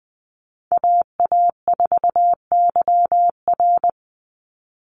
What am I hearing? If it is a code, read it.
Morse code, heard as a keyed tone, about 20 wpm, 700 Hz. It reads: AA4YR
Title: USA Call Signs 20wpm